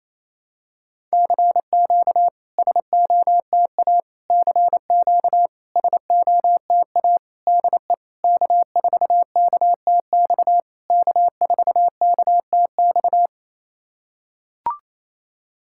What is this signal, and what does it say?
Morse code, heard as a keyed tone, about 28 wpm, 690 Hz. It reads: CQ SOTA CQ SOTA DE K4KTX K4KTX
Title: SOTA QSO Practice 28wpm